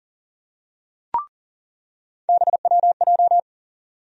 Morse code, heard as a keyed tone, about 40 wpm, 700 Hz. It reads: BWJ
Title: USA Call Sign Postfixes - 3 Letters 40wpm